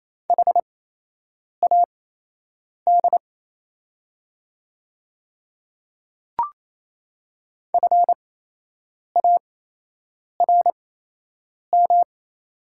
Morse code, heard as a keyed tone, about 28 wpm, spaced as for 8 wpm, 700 Hz. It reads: HAD FARM
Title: 8x Character Spacing - Top 500 Words - 28wpm